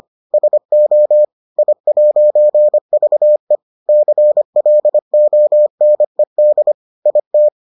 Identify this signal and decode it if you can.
Morse code, heard as a keyed tone, about 25 wpm, 600 Hz. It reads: SO I'VE CLONED IT